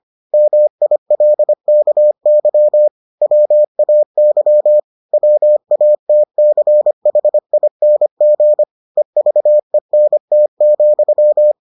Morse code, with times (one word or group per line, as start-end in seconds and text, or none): 0.3-2.9 MILKY
3.2-4.8 WAY
5.1-8.6 WATCHING
9.0-11.5 EVENT,